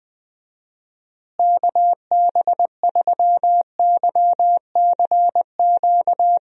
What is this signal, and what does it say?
Morse code, heard as a keyed tone, about 20 wpm, 700 Hz. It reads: KB3YCQ
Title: USA Call Signs - 6 Characters - Repeated 3x - 20wpm